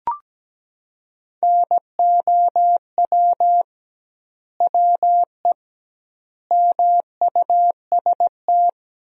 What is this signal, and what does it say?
Morse code, heard as a keyed tone, about 17 wpm, 700 Hz. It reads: NOW WE MUST
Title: 2x Word-Spacing - Sentences from Top 500 Words - 17wpm